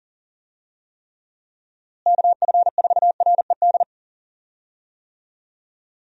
Morse code, heard as a keyed tone, about 39 wpm, 700 Hz. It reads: KF4RED
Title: USA Call Signs - 6 Characters - Repeated 3x - 40wpm